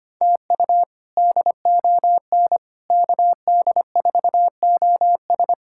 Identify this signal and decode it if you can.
Morse code, heard as a keyed tone, about 25 wpm, 700 Hz. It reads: TU DON KD4OS
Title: K1USN SST QSO Practice - 25wpm